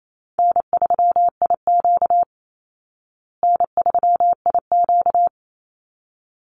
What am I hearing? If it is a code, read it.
Morse code, heard as a keyed tone, about 28 wpm, 700 Hz. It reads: N3IQ N3IQ
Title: Easy Call Signs Repeated 2x - 28wpm